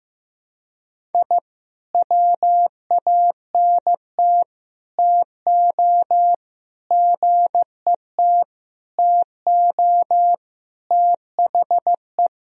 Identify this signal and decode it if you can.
Morse code, heard as a keyed tone, about 15 wpm, 700 Hz. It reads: I WANT TO GET TO THE
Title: Sentences from Top 1000 Words - Review Part 1 - 15wpm